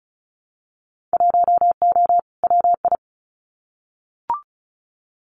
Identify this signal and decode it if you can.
Morse code, heard as a keyed tone, about 36 wpm, 700 Hz. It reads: 1O WI
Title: Winter Field Day Exchanges 35wpm